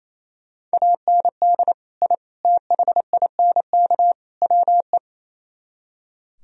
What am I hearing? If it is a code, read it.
Morse code, heard as a keyed tone, about 36 wpm, 700 Hz. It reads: AND I THINK WE